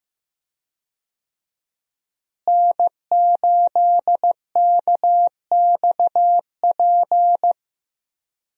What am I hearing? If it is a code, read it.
Morse code, heard as a keyed tone, about 15 wpm, 700 Hz. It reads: N8KXP
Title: USA Call Signs - 5 Characters - Repeated 3x - 15wpm